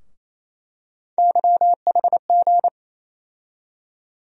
Morse code, heard as a keyed tone, about 27 wpm, 700 Hz. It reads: YHG
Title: USA Call Sign Postfixes - 3 Letters 28wpm